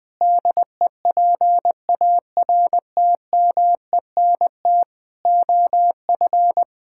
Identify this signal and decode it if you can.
Morse code, heard as a keyed tone, about 21 wpm, 700 Hz. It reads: DEPARTMENT OF
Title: Sentences from Top 1000 Words - Review Part 5 - 20wpm